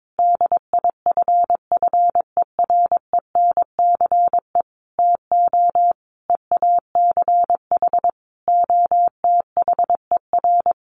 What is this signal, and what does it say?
Morse code, heard as a keyed tone, about 22 wpm, 700 Hz. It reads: DIFFERENCE TO EACH OTHER
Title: Sentences from Top 1500 Words 22wpm